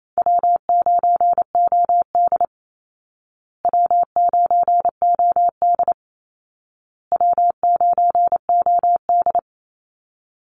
Morse code, heard as a keyed tone, about 28 wpm, 700 Hz. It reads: W9OD W9OD W9OD
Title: Easy Call Signs Repeated 3x - 28wpm